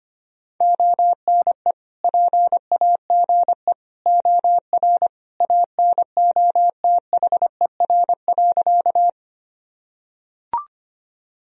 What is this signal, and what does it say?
Morse code, heard as a keyed tone, about 25 wpm, 700 Hz. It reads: ONE PAGE OR ANOTHER.
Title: Sentences from Top 1000 Words - Review Part 1 - 25wpm